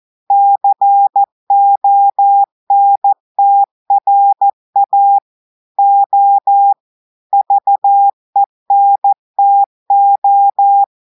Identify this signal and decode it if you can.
Morse code, heard as a keyed tone, about 14 wpm, 800 Hz. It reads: CONTRA O VENTO
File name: audios / portugais / portugais texte en clair 1 vitesse 14 mots minutes